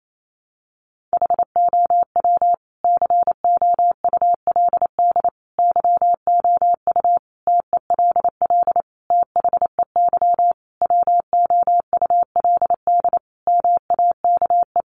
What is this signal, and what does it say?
Morse code, heard as a keyed tone, about 28 wpm, 700 Hz. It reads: HOW COULD YOU TELL THEY WOULD MAKE
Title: Sentences from Top 1000 Words 28wpm